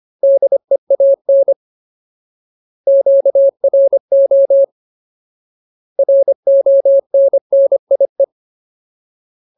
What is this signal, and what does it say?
Morse code, heard as a keyed tone, about 25 wpm, 550 Hz. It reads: DEAN QRO RONNIE
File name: Ultimate Word List / 25WPM Ultimate